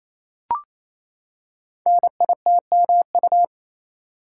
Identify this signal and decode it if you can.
Morse code, heard as a keyed tone, about 27 wpm, 700 Hz. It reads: NITMU
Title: Code Groups - 5 Letter:Numbers - 28wpm